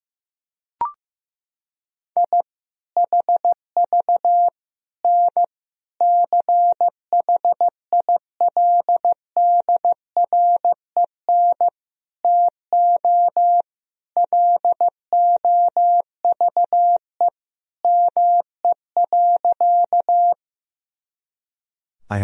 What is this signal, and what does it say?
Morse code, heard as a keyed tone, about 15 wpm, 700 Hz. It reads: I HV N CHILDREN TO LOVE ME.
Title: Sentences from Top 400 Words and Abbreviations 15wpm